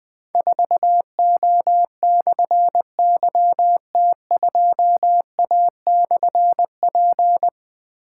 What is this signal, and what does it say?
Morse code, heard as a keyed tone, about 20 wpm, 700 Hz. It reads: 4O/YT2A/P
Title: Difficult Call Signs 20wpm